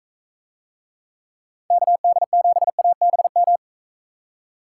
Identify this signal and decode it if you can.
Morse code, heard as a keyed tone, about 41 wpm, 700 Hz. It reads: KD7ABM